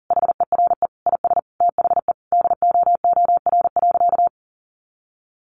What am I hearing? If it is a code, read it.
Morse code, heard as a keyed tone, about 37 wpm, 700 Hz. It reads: HERE IS THE DOOR.
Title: Sentences from Top 1100 Words 40wpm